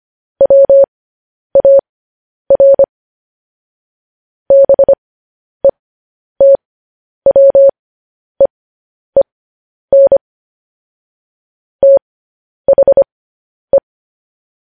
Morse code, heard as a keyed tone, about 25 wpm, spaced as for 10 wpm, 550 Hz. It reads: WAR BETWEEN THE